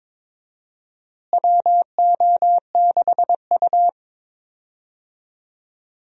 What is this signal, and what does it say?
Morse code, heard as a keyed tone, about 22 wpm, 700 Hz. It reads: WO6U
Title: USA Call Signs - 4 Characters - Repeated 3x - 22wpm